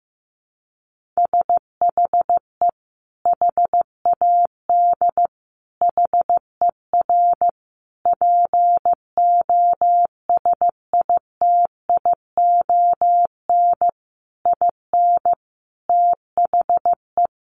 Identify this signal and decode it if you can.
Morse code, heard as a keyed tone, about 15 wpm, 700 Hz. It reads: SHE HAD HER POSITION IN THE
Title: Sentences from Top 900 Words 15wpm